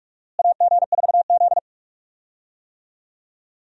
Morse code, heard as a keyed tone, about 44 wpm, 700 Hz. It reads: AG4Z
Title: USA Call Signs - 4 Characters - Repeated 2x - 45wpm